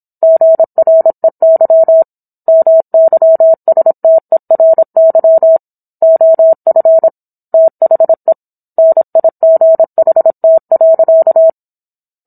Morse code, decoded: GREY MYSTERY OF THE NIGHT.

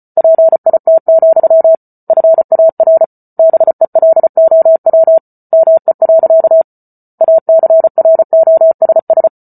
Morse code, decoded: PIT, FAR BELOW ME. ACROSS

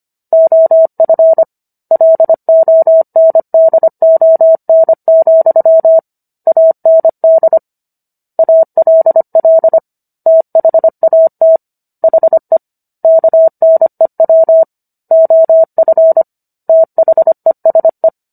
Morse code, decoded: OF LONDON, AND ALL THAT HE KNEW OF THESE

